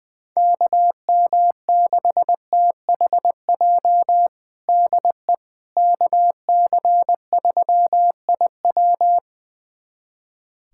Morse code, heard as a keyed tone, about 20 wpm, 700 Hz. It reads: KM6THJ DE KC3IW
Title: USA Callsign DE Callsign 20wpm